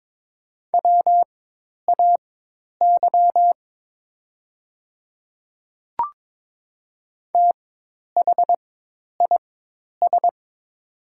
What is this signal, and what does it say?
Morse code, heard as a keyed tone, about 22 wpm, spaced as for 10 wpm, 700 Hz. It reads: WAY THIS
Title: 4x Character Spacing - Top 100 Words - 22wpm